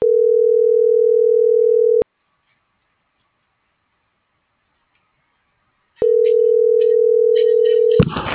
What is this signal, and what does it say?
Ringback tone (North America), 440 + 480 Hz.